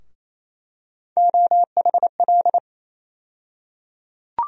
Morse code, heard as a keyed tone, about 28 wpm, 700 Hz. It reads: OHL
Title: USA Call Sign Postfixes - 3 Letters 28wpm